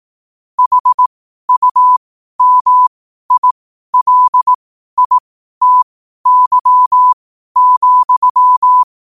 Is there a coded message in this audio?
Yes: Morse code from 0.6 to 8.8 s, heard as a keyed tone, about 18 wpm, spaced as for 12 wpm, 1000 Hz. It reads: HUMILITY,